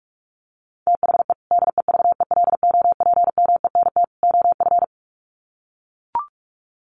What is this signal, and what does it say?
Morse code, heard as a keyed tone, about 44 wpm, 700 Hz. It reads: THE DEVELOPMENT OF
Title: Sets of 3 Words 45wpm